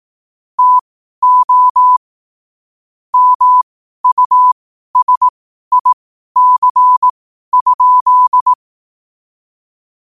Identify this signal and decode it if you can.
Morse code, heard as a keyed tone, about 18 wpm, spaced as for 12 wpm, 1000 Hz. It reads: TO MUSIC?
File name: qotdfiles / 12wpm